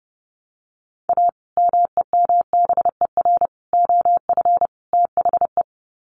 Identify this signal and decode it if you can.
Morse code, heard as a keyed tone, about 30 wpm, 700 Hz. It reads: A MEMBER OF THE
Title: Sets of 4 Words 30wpm